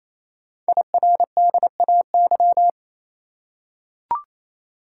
Morse code, heard as a keyed tone, about 28 wpm, 700 Hz. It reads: IRDAY
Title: Code Groups - 5 Letters - 28wpm